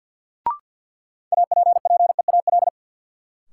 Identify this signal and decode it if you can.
Morse code, heard as a keyed tone, about 47 wpm, 700 Hz. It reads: APPEAL
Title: Sentences from Top 1300 Words - New Word Drill Alt - 50wpm